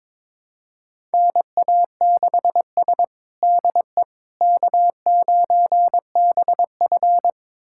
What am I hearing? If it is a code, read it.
Morse code, heard as a keyed tone, about 22 wpm, 700 Hz. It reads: NA6S DE K9BF
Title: USA Callsign DE Callsign - 4 Characters - 22wpm